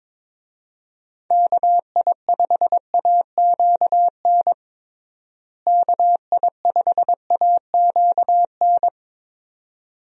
Morse code, heard as a keyed tone, about 22 wpm, 700 Hz. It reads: KI5AQN KI5AQN
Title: USA Call Signs - 6 Characters - Repeated 2x - 22wpm